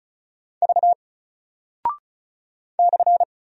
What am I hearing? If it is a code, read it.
Morse code, heard as a keyed tone, about 36 wpm, 700 Hz. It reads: V /
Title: Single Character (Letter:Number:Punctuation:Prosign) Alt 35wpm